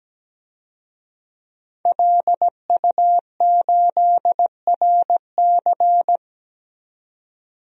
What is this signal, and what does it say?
Morse code, heard as a keyed tone, about 17 wpm, 700 Hz. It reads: LU8RC
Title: DXCC-100 Call Signs - 5 Characters - 17wpm